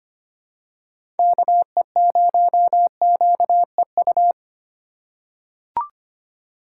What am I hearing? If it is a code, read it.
Morse code, heard as a keyed tone, about 25 wpm, 700 Hz. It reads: KE0QEU
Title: USA Call Signs - 6 Characters - 2x Character Spacing - 25wpm